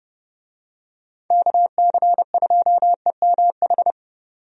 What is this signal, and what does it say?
Morse code, heard as a keyed tone, about 30 wpm, 700 Hz. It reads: KC2EMH
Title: USA Call Signs - 6 Characters - 30wpm